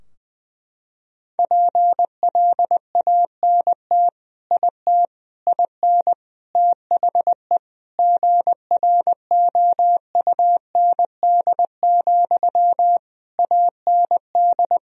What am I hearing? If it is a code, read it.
Morse code, heard as a keyed tone, about 21 wpm, 700 Hz. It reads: PLANT IT IN THE GROUND, AND
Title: Sentences from Top 1000 Words - Review Part 2 - 20wpm